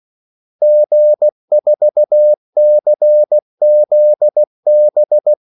Morse code, heard as a keyed tone, about 16 wpm, 600 Hz. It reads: G4CZB
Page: Kaldesignaler 16 wpm